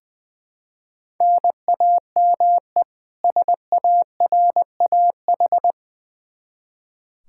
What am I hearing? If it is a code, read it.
Morse code, heard as a keyed tone, about 21 wpm, 700 Hz. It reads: NAME SARAH
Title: K1USN SST QSO Practice 20wpm